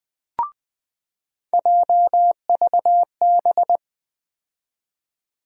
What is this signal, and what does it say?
Morse code, heard as a keyed tone, about 20 wpm, 700 Hz. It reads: JVB
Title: USA Call Sign Postfixes - 3 Letters 20wpm